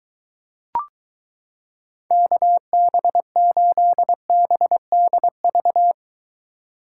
Morse code, heard as a keyed tone, about 22 wpm, 700 Hz. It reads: KB8BDV